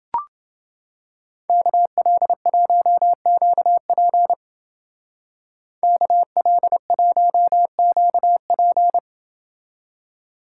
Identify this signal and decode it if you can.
Morse code, heard as a keyed tone, about 28 wpm, 700 Hz. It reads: KL1QP KL1QP